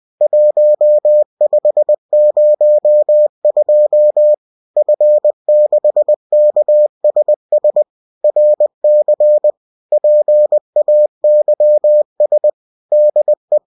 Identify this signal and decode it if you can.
Morse code, heard as a keyed tone, about 20 wpm, 600 Hz. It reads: 1502 F6KSS RC PAYS DE